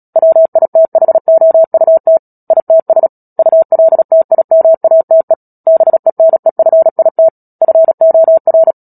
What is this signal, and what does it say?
Morse code, heard as a keyed tone, about 37 wpm, 650 Hz. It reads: WITHOUT ITS ULTIMATE BENEFIT FOR
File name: Book Two / 10-Chapter 10 - Epilogue - 36 WPM